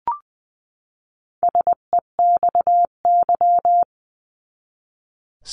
Morse code, heard as a keyed tone, about 20 wpm, 700 Hz. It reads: SEXY
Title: 259 Lesson 42d - Words 18-20wpm